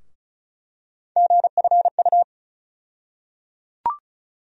Morse code, heard as a keyed tone, about 36 wpm, 700 Hz. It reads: GFU
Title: USA Call Sign Postfixes - 3 Letters 35wpm